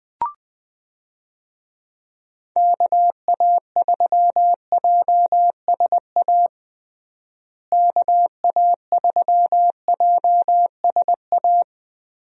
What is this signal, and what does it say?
Morse code, heard as a keyed tone, about 21 wpm, 700 Hz. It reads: KA3JSA KA3JSA
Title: USA Call Signs - 6 Characters - Repeated 3x - 20wpm